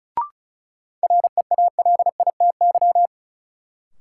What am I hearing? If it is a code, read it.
Morse code, heard as a keyed tone, about 36 wpm, 700 Hz. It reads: REALITY
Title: Sentences from Top 1200 Words - New Word Drill - 35wpm